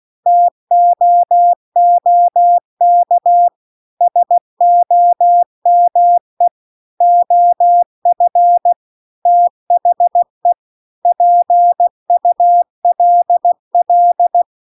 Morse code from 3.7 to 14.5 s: SOME OF THE PULL